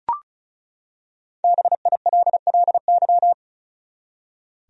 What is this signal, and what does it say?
Morse code, heard as a keyed tone, about 35 wpm, 700 Hz. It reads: BILLY
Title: USA Common Names - Max 5 Letters - 35wpm